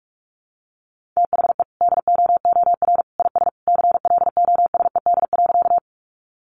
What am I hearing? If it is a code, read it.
Morse code, heard as a keyed tone, about 44 wpm, 700 Hz. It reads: THE DOOR IS CLOSED.
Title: Sentences from Top 1400 Words 45wpm